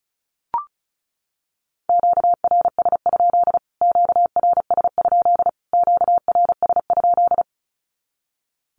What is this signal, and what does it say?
Morse code, heard as a keyed tone, about 34 wpm, 700 Hz. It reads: QRS? QRS? QRS?
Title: Q-Codes Repeated 3x 35wpm